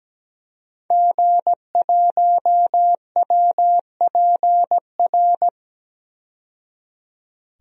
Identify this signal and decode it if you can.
Morse code, heard as a keyed tone, about 17 wpm, 700 Hz. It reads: G1WPR
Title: DXCC-100 Call Signs - 5 Characters - 17wpm